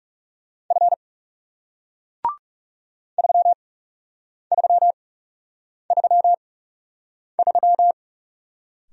Morse code, heard as a keyed tone, about 38 wpm, spaced as for 8 wpm, 700 Hz. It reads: F 3333